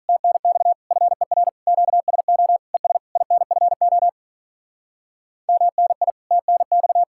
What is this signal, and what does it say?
Morse code, heard as a keyed tone, about 42 wpm, 700 Hz. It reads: TNX FER QSO ES INFO MNI TNX
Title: Speed-Racing Common QSO Phrases - 35wpm